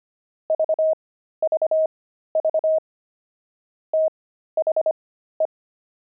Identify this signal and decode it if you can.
Morse code, heard as a keyed tone, about 25 wpm, spaced as for 13 wpm, 630 Hz. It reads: VVV THE
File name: pangram-fs-13-cs-25-pigeons